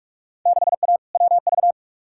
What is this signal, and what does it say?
Morse code, heard as a keyed tone, about 46 wpm, 700 Hz. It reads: 6A WV
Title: Field Day Exchanges - 45wpm